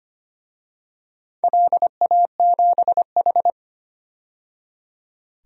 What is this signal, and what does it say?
Morse code, heard as a keyed tone, about 25 wpm, 700 Hz. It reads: LA7H
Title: DXCC-100 Call Signs - 4 Characters - 25wpm